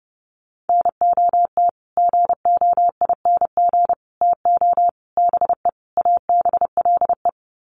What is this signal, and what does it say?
Morse code, heard as a keyed tone, about 30 wpm, 700 Hz. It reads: NOT GOING TO BE ABLE
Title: Sets of 5 Words 30wpm